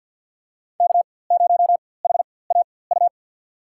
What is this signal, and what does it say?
Morse code, heard as a keyed tone, about 53 wpm, spaced as for 24 wpm, 700 Hz. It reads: K0HAU
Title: USA Call Signs - 5 Characters - 4x Character Spacing - 50wpm